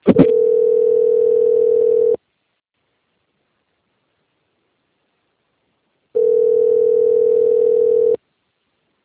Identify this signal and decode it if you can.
Ringback tone (North America), 440 + 480 Hz.
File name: out- - - - - .62098